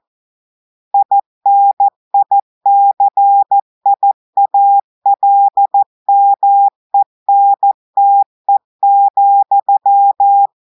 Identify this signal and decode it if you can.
Morse code, heard as a keyed tone, about 14 wpm, 800 Hz. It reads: INICIALMENTE,
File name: audios / portugais / portugais texte en clair 2 vitesse 14 mots minutes